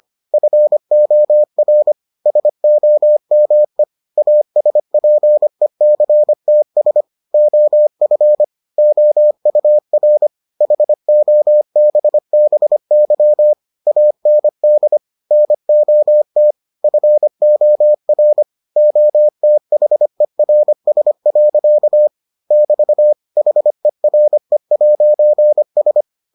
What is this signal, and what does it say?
Morse code, heard as a keyed tone, about 25 wpm, 600 Hz. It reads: FOR SOME ASPECTS OF OUR HOBBY AND NOT FOR OTHERS. <BT> HERE'S